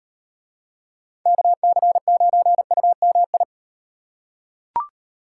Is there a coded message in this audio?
Yes: Morse code from 1.3 to 3.4 s, heard as a keyed tone, about 36 wpm, 700 Hz. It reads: KC9UMI